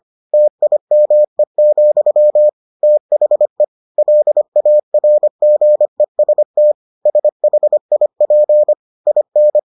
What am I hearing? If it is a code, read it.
Morse code, heard as a keyed tone, about 25 wpm, 600 Hz. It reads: TIME, THE LARGEST SHIP IN